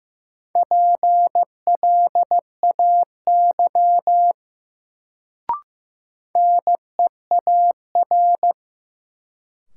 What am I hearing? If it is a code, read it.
Morse code, heard as a keyed tone, about 15 wpm, 700 Hz. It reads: PLAY NEAR
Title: Sentences from Top 200 Words - New Word Drill - 15wpm